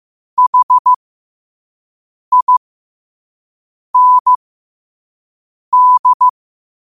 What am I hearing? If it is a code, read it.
Morse code, heard as a keyed tone, about 15 wpm, spaced as for 5 wpm, 1000 Hz. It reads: HIND
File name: qotdfiles / 5wpm